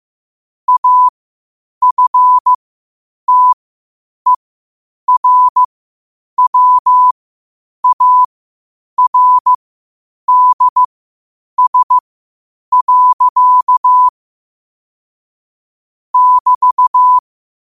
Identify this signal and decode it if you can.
Morse code, heard as a keyed tone, about 15 wpm, spaced as for 8 wpm, 1000 Hz. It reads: AFTERWARDS. <BT>
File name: qotdfiles / 8wpm